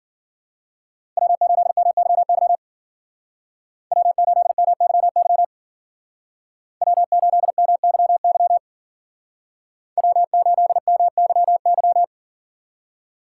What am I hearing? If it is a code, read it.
Morse code, heard as a keyed tone, about 52 wpm, 700 Hz. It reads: W8MYY W8MYY W8MYY W8MYY